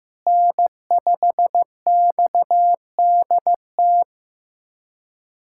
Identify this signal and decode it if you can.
Morse code, heard as a keyed tone, about 15 wpm, 700 Hz. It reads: N5XDT
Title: USA Callsign DE Callsign - 5 Characters - 15wpm